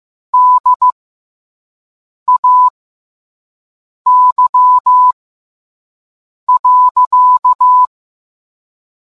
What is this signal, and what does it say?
Morse code, heard as a keyed tone, about 15 wpm, spaced as for 5 wpm, 1000 Hz. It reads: DAY.